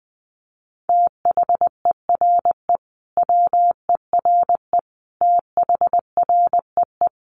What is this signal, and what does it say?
Morse code, heard as a keyed tone, about 20 wpm, 700 Hz. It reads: THERE WERE THREE
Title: Sentences from Top 800 Words 20wpm